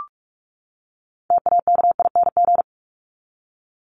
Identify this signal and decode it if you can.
Morse code, heard as a keyed tone, about 44 wpm, 700 Hz. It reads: TAKING
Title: ING Suffix - Top 100 Words with Base Words 45wpm